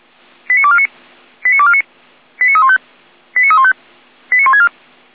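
Five-tone selective call (ZVEI), heard as quick runs of five tones, tones of about 70 ms. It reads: 89249 89249 89315 89315 89164
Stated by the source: Five-tone ZVEI selective call 89570 received by SDR, channel 455.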